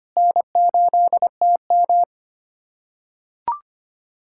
Morse code, heard as a keyed tone, about 25 wpm, 700 Hz. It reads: N8TM
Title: SOTA QSO Practice 25wpm